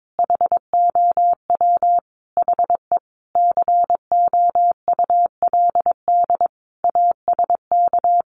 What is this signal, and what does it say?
Morse code, heard as a keyed tone, about 22 wpm, 700 Hz. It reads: HOW HE COULD ASK
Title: Sentences from Top 600 Words 22wpm